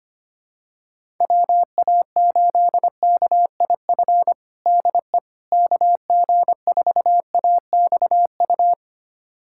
Morse code, heard as a keyed tone, about 25 wpm, 700 Hz. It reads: WA8KIF DE KG4AXU
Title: USA Callsign DE Callsign - 6 Characters - 25wpm